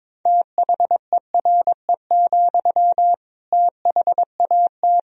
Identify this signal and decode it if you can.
Morse code, heard as a keyed tone, about 22 wpm, 700 Hz. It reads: THERE, THAT
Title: Sentences from Top 200 Words and Abbreviations 22wpm